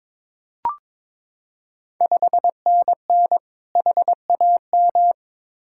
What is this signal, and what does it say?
Morse code, heard as a keyed tone, about 22 wpm, 690 Hz. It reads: 5NN HAM
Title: Ontario QSO Party Exchange 22wpm